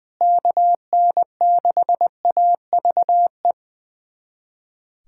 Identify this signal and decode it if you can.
Morse code, heard as a keyed tone, about 20 wpm, 700 Hz. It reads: KN6AVE
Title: USA Callsign DE Callsign - 6 Characters - 20wpm